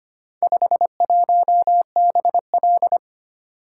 Morse code, heard as a keyed tone, about 25 wpm, 700 Hz. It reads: 51BL
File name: practice / 25wpm / Four Letter-Number Combinations - 25wpm